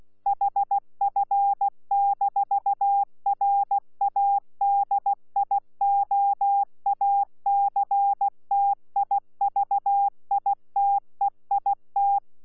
Morse code, heard as a keyed tone, about 16 wpm, 800 Hz. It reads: HF-RADIOACTIVITEIT
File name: Z PI4VRZ / PI4VRZ 14-jun-2025